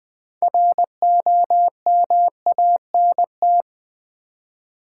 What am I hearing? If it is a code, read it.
Morse code, heard as a keyed tone, about 20 wpm, 700 Hz. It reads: ROMANT
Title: Build A Word - 3 Letters At A Time - 8-Letter Words - 20wpm